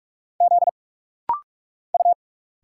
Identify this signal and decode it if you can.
Morse code, heard as a keyed tone, about 46 wpm, 690 Hz. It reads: Z U
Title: Single Letters - Rapid-Fire 45wpm